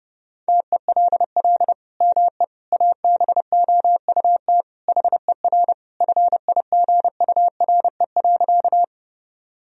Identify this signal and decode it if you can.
Morse code, heard as a keyed tone, about 29 wpm, 700 Hz. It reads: TELL ME ABOUT HER FIGURE.
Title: Sentences from Top 1000 Words - Review Part 2 - 30wpm